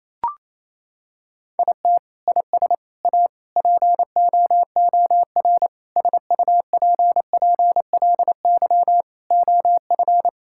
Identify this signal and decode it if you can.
Morse code, heard as a keyed tone, about 28 wpm, 700 Hz. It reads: IT IS A POOR SUPPLY OF
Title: Sentences from Top 1000 Words - Review Part 5 - 28wpm